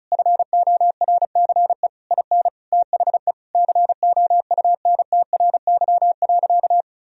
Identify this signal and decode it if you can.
Morse code, heard as a keyed tone, about 34 wpm, 700 Hz. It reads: FORCE IN THE COUNTRY.
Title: Sentences from Top 1000 Words - Review Part 4 - 35wpm